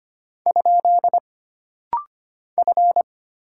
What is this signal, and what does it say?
Morse code, heard as a keyed tone, about 25 wpm, 700 Hz. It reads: ? F
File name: Single Character (Letter:Number:Punctuation:Prosign) - Rapid-Fire - 25wpm